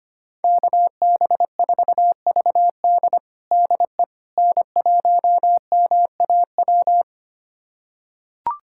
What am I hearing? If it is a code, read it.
Morse code, heard as a keyed tone, about 25 wpm, 700 Hz. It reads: KB4VD DE N1MAW
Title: USA Callsign DE Callsign - 5 Characters - 25wpm